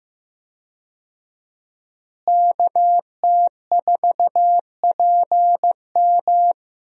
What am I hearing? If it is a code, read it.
Morse code, heard as a keyed tone, about 15 wpm, 700 Hz. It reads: KT4PM
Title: USA Call Signs - 5 Characters - Repeated 2x - 15wpm